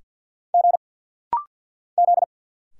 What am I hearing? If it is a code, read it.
Morse code, heard as a keyed tone, about 50 wpm, 700 Hz. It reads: G Z
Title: Single Letter-Number Alt - Rapid-Fire 50wpm